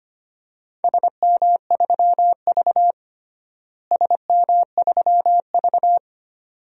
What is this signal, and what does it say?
Morse code, heard as a keyed tone, about 25 wpm, 700 Hz. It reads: SM3V SM3V
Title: DXCC-100 Call Signs - 4 Characters - Repeated 2x - 25wpm